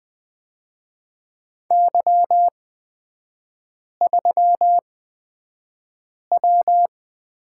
Morse code, heard as a keyed tone, about 20 wpm, spaced as for 5 wpm, 700 Hz. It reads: Y3W
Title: 243 Lesson 39e - Letter z - Three Characters 20wpm